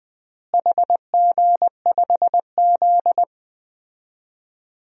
Morse code, heard as a keyed tone, about 21 wpm, 700 Hz. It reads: HG5Z
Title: DXCC-100 Call Signs - 4 Characters - Repeated 2x - 20wpm